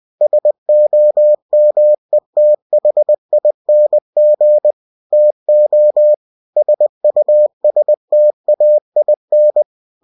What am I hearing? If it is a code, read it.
Morse code, heard as a keyed tone, about 20 wpm, 600 Hz. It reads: SOMETHING TO SUSTAIN
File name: cw-books / winnie-the-pooh / winnie-20wpm-page-0032